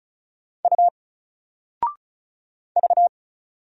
Morse code, heard as a keyed tone, about 36 wpm, 700 Hz. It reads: U V
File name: Single-Letter-Number-35wpm